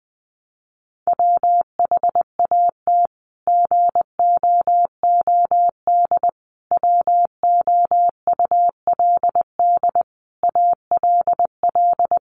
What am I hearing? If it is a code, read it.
Morse code, heard as a keyed tone, about 20 wpm, 700 Hz. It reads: WHAT GOOD WOULD ALL
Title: Sentences from Top 1200 Words 20wpm